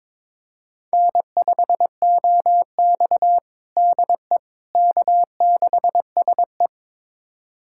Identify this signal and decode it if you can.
Morse code, heard as a keyed tone, about 21 wpm, 700 Hz. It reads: N5OX DE K6SE
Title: USA Callsign DE Callsign - 4 Characters - 22wpm